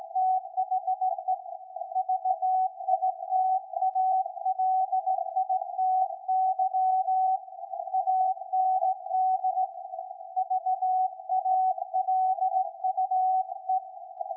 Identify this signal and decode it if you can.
Morse code, heard as a keyed tone, about 17 wpm, 750 Hz. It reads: THE VITALITY AND VALUE
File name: iz7slz / VLF / SAQ 13-02-2023